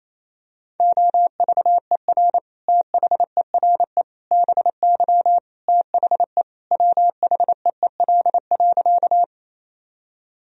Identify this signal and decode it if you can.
Morse code, heard as a keyed tone, about 28 wpm, 700 Hz. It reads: OVER THERE BY THE WHEEL.
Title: Sentences from Top 1000 Words - Review Part 3 - 28wpm